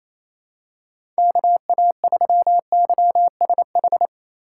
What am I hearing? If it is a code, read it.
Morse code, heard as a keyed tone, about 28 wpm, 700 Hz. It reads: KA3YSH
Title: USA Call Signs - 6 Characters - 2x Character Spacing - 28wpm